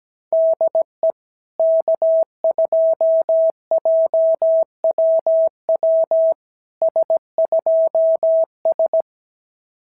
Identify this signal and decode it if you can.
Morse code, heard as a keyed tone, about 17 wpm, 650 Hz. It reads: DE K2JWW S2S
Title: SOTA QSO Practice 17wpm